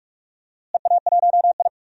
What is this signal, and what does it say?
Morse code, heard as a keyed tone, about 45 wpm, 700 Hz. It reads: EA1I
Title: DXCC-100 Call Signs - 4 Characters - Repeated 3x - 45wpm